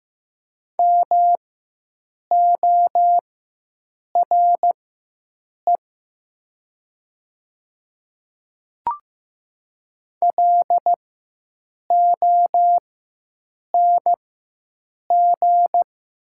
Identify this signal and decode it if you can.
Morse code, heard as a keyed tone, about 15 wpm, spaced as for 7 wpm, 700 Hz. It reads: MORE LONG
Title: 4x Character Spacing - Top 100 Words - 15wpm